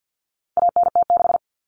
Morse code, heard as a keyed tone, about 50 wpm, 700 Hz. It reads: ANT6
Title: Common Cut Numbers - 4-Digit 50wpm